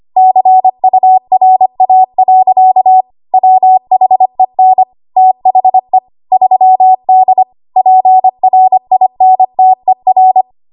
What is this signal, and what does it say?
Morse code, heard as a keyed tone, about 25 wpm, 750 Hz. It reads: CURA. WHEN THE 3D PRINTER
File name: Archive / 25 WPM / 25WPM